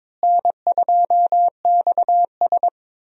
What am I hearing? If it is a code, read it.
Morse code, heard as a keyed tone, about 22 wpm, 700 Hz. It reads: N2XS
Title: USA Call Signs - 4 Characters - Repeated 2x - 22wpm